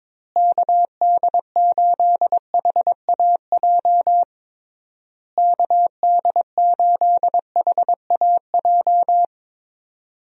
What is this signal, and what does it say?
Morse code, heard as a keyed tone, about 22 wpm, 700 Hz. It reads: KD8HAJ KD8HAJ
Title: USA Call Signs - 6 Characters - Repeated 3x - 22wpm